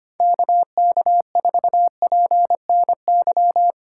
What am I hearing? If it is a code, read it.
Morse code, heard as a keyed tone, about 25 wpm, 690 Hz. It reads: KK4PNY
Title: USA Call Signs - 6 Characters - 3x Character Spacing - 25wpm